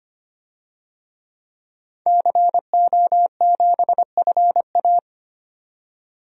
Morse code, heard as a keyed tone, about 25 wpm, 700 Hz. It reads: CO7FA
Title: DXCC-100 Call Signs - 5 Characters - 25wpm